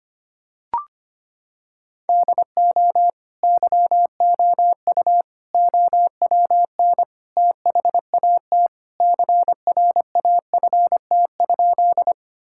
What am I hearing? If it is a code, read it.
Morse code, heard as a keyed tone, about 25 wpm, 700 Hz. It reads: DO YOU OWN THAT CRAFT?
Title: Sentences from Top 1000 Words - Review Part 6 - 25wpm